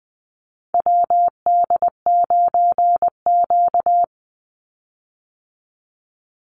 Morse code, heard as a keyed tone, about 20 wpm, 700 Hz. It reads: WD9Q
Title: Call Signs - 20wpm